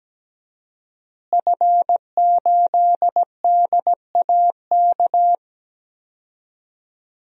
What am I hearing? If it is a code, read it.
Morse code, heard as a keyed tone, about 17 wpm, 700 Hz. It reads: F8DAK
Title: DXCC-100 Call Signs - 5 Characters - Repeated 3x - 17wpm